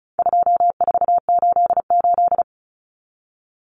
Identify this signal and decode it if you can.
Morse code, heard as a keyed tone, about 34 wpm, 700 Hz. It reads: 2488
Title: Numbers - 4-Digit 35wpm